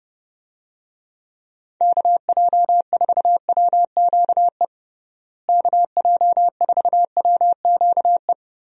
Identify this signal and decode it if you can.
Morse code, heard as a keyed tone, about 30 wpm, 700 Hz. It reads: KJ4WQE KJ4WQE
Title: USA Call Signs - 6 Characters - Repeated 2x - 30wpm